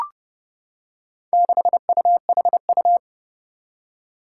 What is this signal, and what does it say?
Morse code, heard as a keyed tone, about 30 wpm, 700 Hz. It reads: 6UHU
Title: Four Challenging Letter-Number Combinations 30wpm